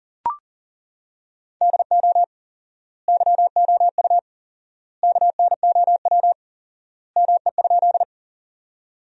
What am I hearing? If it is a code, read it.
Morse code, heard as a keyed tone, about 40 wpm, 700 Hz. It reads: DO YOU KNOW ME?
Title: 4x Word-Spacing - Sentences from Top 200 Words - 40wpm